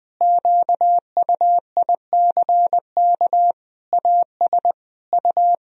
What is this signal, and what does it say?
Morse code, heard as a keyed tone, about 21 wpm, 700 Hz. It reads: QUICK AS U
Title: Sentences from Top 500 Words and Abbreviations 20wpm